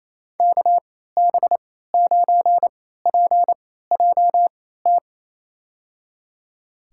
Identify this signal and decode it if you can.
Morse code, heard as a keyed tone, about 28 wpm, spaced as for 16 wpm, 700 Hz. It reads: KB9PJT
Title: USA Call Signs - 3x Character Spacing - 28wpm